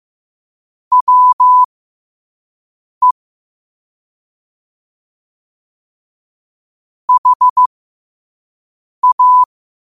Morse code, heard as a keyed tone, about 15 wpm, spaced as for 5 wpm, 1000 Hz. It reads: WE HA
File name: qotdfiles / 5wpm